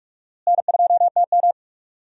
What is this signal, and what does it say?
Morse code, heard as a keyed tone, about 41 wpm, 700 Hz. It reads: N2TM
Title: USA Call Signs - 4 Characters - Repeated 3x - 45wpm